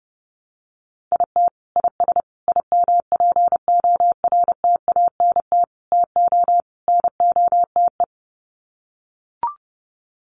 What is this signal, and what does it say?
Morse code, heard as a keyed tone, about 30 wpm, 700 Hz. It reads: IT IS IMPORTANT TO NOTE
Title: Sets of 5 Words 30wpm